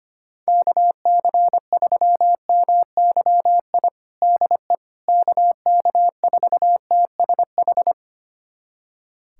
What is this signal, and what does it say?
Morse code, heard as a keyed tone, about 25 wpm, 700 Hz. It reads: KC3MYI DE KK4TSH
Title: USA Callsign DE Callsign - 6 Characters - 25wpm